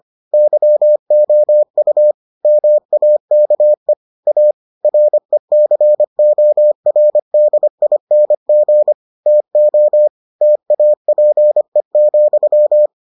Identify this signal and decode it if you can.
Morse code, heard as a keyed tone, about 25 wpm, 600 Hz. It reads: YOU MAKE A RECORDING TO TAPE,